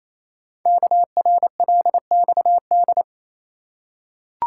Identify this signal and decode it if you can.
Morse code, heard as a keyed tone, about 27 wpm, 700 Hz. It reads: KRLXD
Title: Code Groups - 5 Letters - 28wpm